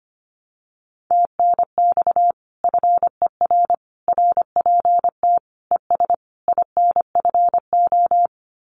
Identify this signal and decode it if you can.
Morse code, heard as a keyed tone, about 25 wpm, 700 Hz. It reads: TNX FER RPT ES INFO
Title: Common QSO Phrases Encore 25wpm